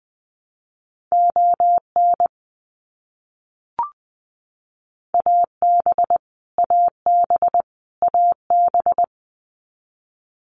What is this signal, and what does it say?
Morse code, heard as a keyed tone, about 20 wpm, 700 Hz. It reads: ON AB AB AB
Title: Canadian Provinces and Territories - Repeated 3x - 20wpm